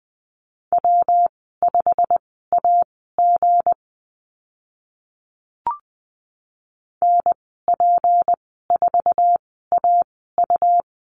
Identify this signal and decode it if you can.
Morse code, heard as a keyed tone, about 20 wpm, spaced as for 14 wpm, 700 Hz. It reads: W5AG NP4AU
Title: 179 Lesson 29g - Slash - Call Signs 20wpm